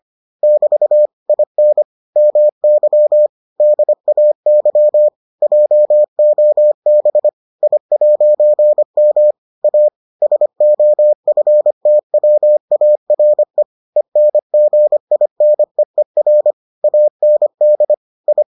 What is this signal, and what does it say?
Morse code, heard as a keyed tone, about 25 wpm, 600 Hz. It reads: <BT> IN MY DAY JOB I'M A SOFTWARE ENGINEER AND I